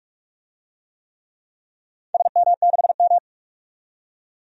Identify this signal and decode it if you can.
Morse code, heard as a keyed tone, about 47 wpm, 700 Hz. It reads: SM6M
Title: DXCC-100 Call Signs - 4 Characters - 45wpm